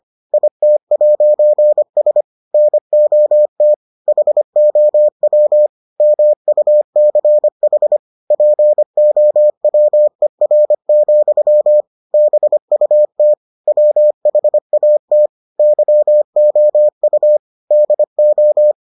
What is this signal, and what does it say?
Morse code, heard as a keyed tone, about 25 wpm, 600 Hz. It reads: IT'S NOT HOW MUCH POWER, BUT WHAT YOU DO